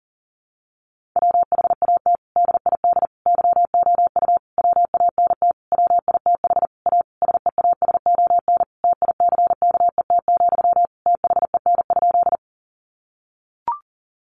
Morse code, heard as a keyed tone, about 40 wpm, 700 Hz. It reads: WHAT DID YOU WANT WITH A SEASON TICKET, THEN?
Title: Sentences from Top 1400 Words 40wpm